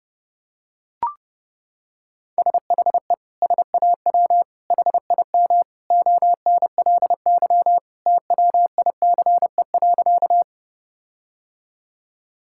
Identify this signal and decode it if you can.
Morse code, heard as a keyed tone, about 32 wpm, 700 Hz. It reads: SHE SAW HIM ONLY TWICE.
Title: Sentences from Top 1600 Words 30wpm